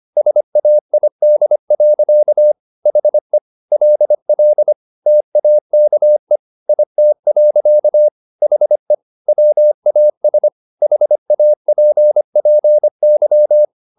Morse code, heard as a keyed tone, about 25 wpm, 600 Hz. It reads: SAID. HE LL TAKE IT. HE WAS HAPPY
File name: the old man and the sea part 1 25wpm